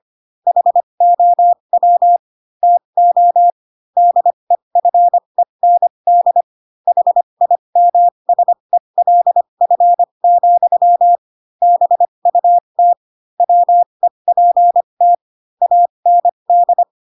Morse code, heard as a keyed tone, about 25 wpm, spaced as for 22 wpm, 700 Hz. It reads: HOW TO DEFEND HIMSELF, BUT WEPT AND
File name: Pinocchio / Pinocchio ch3 18 25 ik1hge